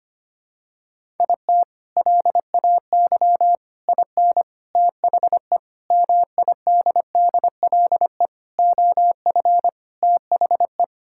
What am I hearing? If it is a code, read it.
Morse code, heard as a keyed tone, about 25 wpm, 700 Hz. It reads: IT LAY IN THE MIDDLE OF THE
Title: Sentences from Top 1000 Words - Review Part 6 - 25wpm